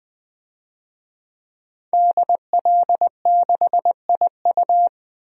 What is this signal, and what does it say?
Morse code, heard as a keyed tone, about 20 wpm, 700 Hz. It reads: DL6IU
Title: DXCC-100 Call Signs - 5 Characters - 20wpm